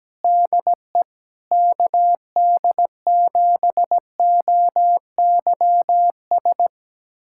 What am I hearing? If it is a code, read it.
Morse code, heard as a keyed tone, about 17 wpm, 700 Hz. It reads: DE KD7OYS
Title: USA Callsign DE Callsign - 6 Characters - 17wpm